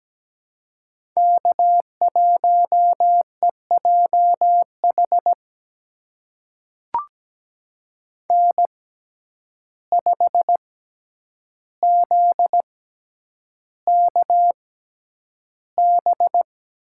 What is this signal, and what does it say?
Morse code, heard as a keyed tone, about 17 wpm, spaced as for 9 wpm, 700 Hz. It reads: K1EJH N5ZKB
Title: USA Call Signs - 5 Characters - 6x Character Spacing - 17wpm